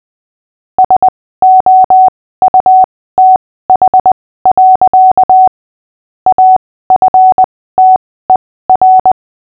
Morse code, heard as a keyed tone, about 20 wpm, spaced as for 15 wpm, 750 Hz. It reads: SOUTH. AFTER
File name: cw / fists / Short-Story-5-15wpm